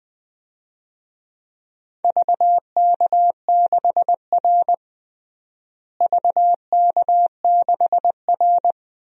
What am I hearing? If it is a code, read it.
Morse code, heard as a keyed tone, about 20 wpm, 700 Hz. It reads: VK6R VK6R
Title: DXCC-100 Call Signs - 4 Characters - Repeated 2x - 20wpm